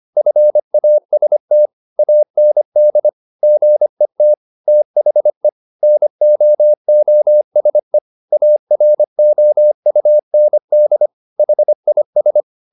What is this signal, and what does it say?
Morse code, heard as a keyed tone, about 25 wpm, 600 Hz. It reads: FAST AND GET THE NOOSE AROUND HIS